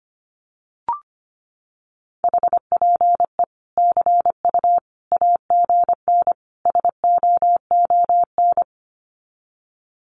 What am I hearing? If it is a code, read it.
Morse code, heard as a keyed tone, about 25 wpm, 700 Hz. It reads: HPE CU AGN SOON
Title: Common QSO Phrases 25wpm